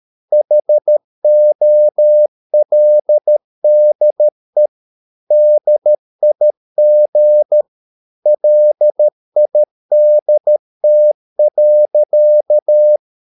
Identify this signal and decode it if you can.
Morse code, heard as a keyed tone, about 13 wpm, 600 Hz. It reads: HOLDE DIG LIDT.
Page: Ferie 13 wpm